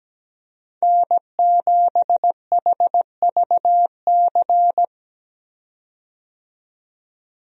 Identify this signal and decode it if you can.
Morse code, heard as a keyed tone, about 17 wpm, 700 Hz. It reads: N7HVC
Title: USA Call Signs - 5 Characters - 17wpm